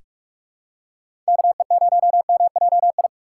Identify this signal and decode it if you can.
Morse code, heard as a keyed tone, about 45 wpm, 700 Hz. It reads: KE0MJI
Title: USA Call Signs - 3x Character Spacing - 45wpm